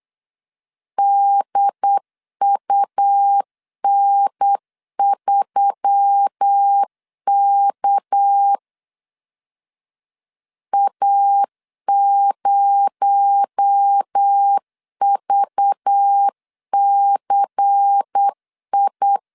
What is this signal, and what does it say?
Morse code, heard as a keyed tone, about 8 wpm, 800 Hz. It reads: DUN3K A0VCI
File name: CW / DET / 1 6 1500C S B 8wpm 5142